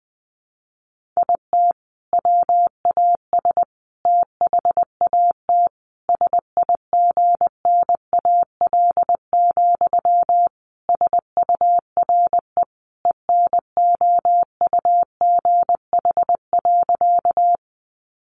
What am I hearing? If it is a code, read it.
Morse code, heard as a keyed tone, about 20 wpm, 700 Hz. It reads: IT WAS THAT SIGNAL, SURE ENOUGH.
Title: Sentences from Top 1400 Words 20wpm